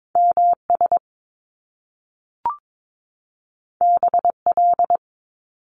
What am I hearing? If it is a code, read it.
Morse code, heard as a keyed tone, about 22 wpm, 700 Hz. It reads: M S B L
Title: Top Two Letter Combinations in Words 22wpm